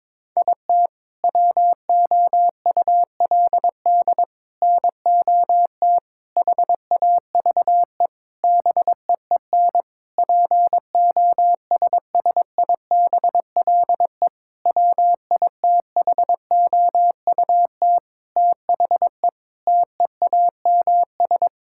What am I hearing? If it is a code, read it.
Morse code, heard as a keyed tone, about 22 wpm, 700 Hz. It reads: IT WOULD NOT HAVE BEEN POSSIBLE WITHOUT THE TEAMS
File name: Sentences with 7-Word Phrases - 22wpm